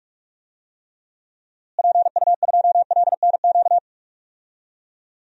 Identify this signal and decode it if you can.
Morse code, heard as a keyed tone, about 41 wpm, 700 Hz. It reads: PU2LNQ
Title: DXCC-100 Call Signs - 6 Characters - Repeated 2x - 45wpm